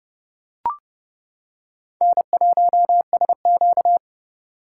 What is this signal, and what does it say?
Morse code, heard as a keyed tone, about 28 wpm, 700 Hz. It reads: N1SQ